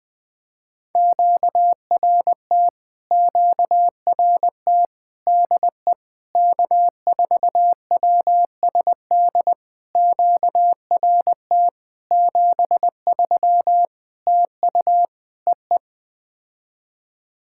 Morse code, heard as a keyed tone, about 20 wpm, 700 Hz. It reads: QRT QRT DE K4WSD QRT 73 TU EE
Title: POTA QSO Practice - 20wpm